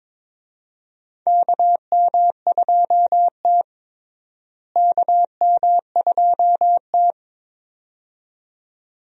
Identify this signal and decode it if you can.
Morse code, heard as a keyed tone, about 22 wpm, 700 Hz. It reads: KM2T KM2T
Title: USA Call Signs - 4 Characters - Repeated 2x - 22wpm